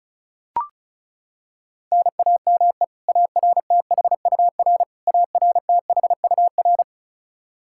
Morse code, heard as a keyed tone, about 36 wpm, 700 Hz. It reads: NAME ARTHUR ARTHUR
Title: Practice QSO - Exchange 1 - Beginning - 35wpm